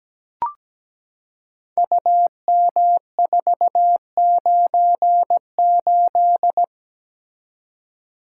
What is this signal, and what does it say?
Morse code, heard as a keyed tone, about 17 wpm, 700 Hz. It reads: UM498
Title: Code Groups - 5 Letter:Numbers - 17wpm